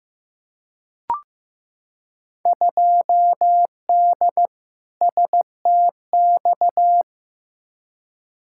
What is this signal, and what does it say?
Morse code, heard as a keyed tone, about 15 wpm, 700 Hz. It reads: 2D STX
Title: Field Day Exchanges - 15wpm